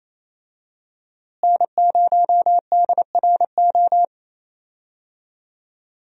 Morse code, heard as a keyed tone, about 28 wpm, 700 Hz. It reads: N0DRO
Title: USA Call Signs - 5 Characters - Repeated 3x - 28wpm